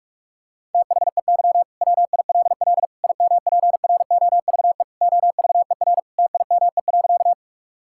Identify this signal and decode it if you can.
Morse code, heard as a keyed tone, about 41 wpm, 690 Hz. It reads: THEY WILL IMPROVE OVER TIME.
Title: Sentences from Top 1000 Words - Review Part 5 - 45wpm